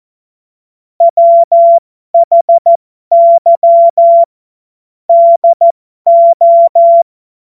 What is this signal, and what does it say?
Morse code, heard as a keyed tone, about 14 wpm, spaced as for 12 wpm, 680 Hz. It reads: WHY DO
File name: Morsekurs / Witze